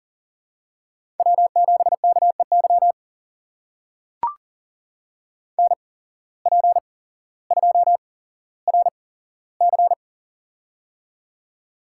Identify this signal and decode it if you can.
Morse code, heard as a keyed tone, about 40 wpm, spaced as for 17 wpm, 700 Hz. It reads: W7KEY NP2RC
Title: USA Call Signs - 5 Characters - 8x Character Spacing - 40wpm